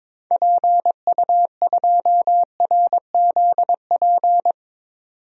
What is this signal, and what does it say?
Morse code, heard as a keyed tone, about 22 wpm, 700 Hz. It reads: PU2RZP
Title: DXCC-100 Call Signs - 6 Characters - 22wpm